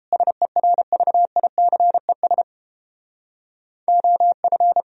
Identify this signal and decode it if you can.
Morse code, heard as a keyed tone, about 29 wpm, 700 Hz. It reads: SERVICES OF